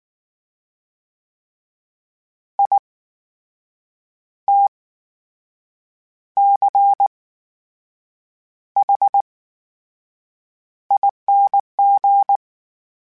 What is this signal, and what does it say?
Morse code, heard as a keyed tone, about 19 wpm, spaced as for 6 wpm, 800 Hz. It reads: ITCHING
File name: Special INGs / Special INGs-mp3s / VerySlowToFaster Ing @18